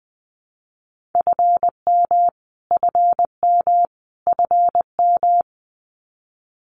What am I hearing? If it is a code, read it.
Morse code, heard as a keyed tone, about 19 wpm, 700 Hz. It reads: FM FM FM
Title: Abbreviations Repeated 3x - 20wpm